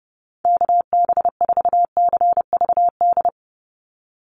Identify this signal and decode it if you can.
Morse code, heard as a keyed tone, about 32 wpm, 700 Hz. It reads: KB4CVD
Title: USA Call Signs 30wpm